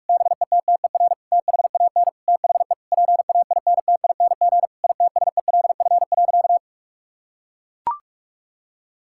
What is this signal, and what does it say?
Morse code, heard as a keyed tone, about 41 wpm, 700 Hz. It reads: BETTER THAN THE PAINTING ITSELF.
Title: Sentences from Top 1000 Words - Review Part 6 - 45wpm